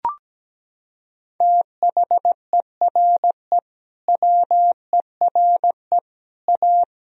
Morse code, heard as a keyed tone, about 17 wpm, 700 Hz. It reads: THERE WERE A
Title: Sentences from Top 1000 Words - Review Part 1 - 17wpm